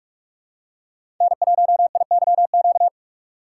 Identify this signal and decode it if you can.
Morse code, heard as a keyed tone, about 44 wpm, 700 Hz. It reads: N1IYQ
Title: USA Call Signs - 4x Character Spacing - 45wpm